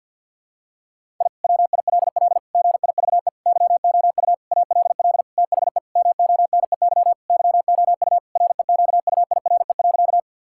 Morse code, heard as a keyed tone, about 51 wpm, 700 Hz. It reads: I WILL GIVE YOU ALL THE MONEY YOU REQUIRE.
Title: Sentences from Top 1000 Words - Review Part 6 - 50wpm